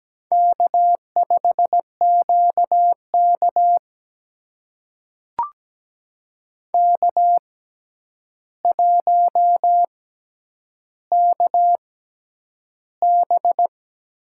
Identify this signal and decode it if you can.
Morse code, heard as a keyed tone, about 17 wpm, spaced as for 9 wpm, 700 Hz. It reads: K5QK K1KB
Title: USA Call Signs - 4 Characters - 6x Character Spacing - 17wpm